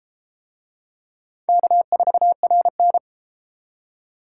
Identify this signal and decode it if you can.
Morse code, heard as a keyed tone, about 31 wpm, 700 Hz. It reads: K4RN